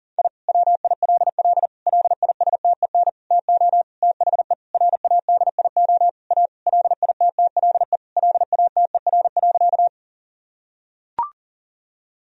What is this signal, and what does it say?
Morse code, heard as a keyed tone, about 40 wpm, 700 Hz. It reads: I WILL LISTEN TO THE RADIO A LITTLE LATER.
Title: Sentences from Top 1000 Words - Review Part 3 - 40wpm